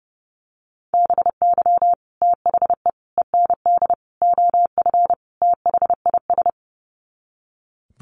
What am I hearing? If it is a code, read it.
Morse code, heard as a keyed tone, about 30 wpm, 700 Hz. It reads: BY THE END OF THIS
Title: Sets of 5 Words Encore 30wpm